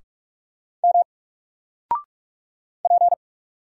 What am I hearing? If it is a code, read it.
Morse code, heard as a keyed tone, about 44 wpm, 700 Hz. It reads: M P
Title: Single Character (Letter:Number:Punctuation:Prosign) 45wpm